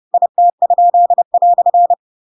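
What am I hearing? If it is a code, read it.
Morse code, heard as a keyed tone, about 30 wpm, 700 Hz. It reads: IT?"
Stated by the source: Chapter 2 of The Adventures of Pinocchio read as a book in Morse code (CW) at 30 wpm, by ik1hge.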